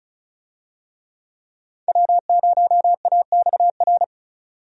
Morse code, heard as a keyed tone, about 36 wpm, 700 Hz. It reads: W0AXR
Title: USA Call Signs - 5 Characters - 35wpm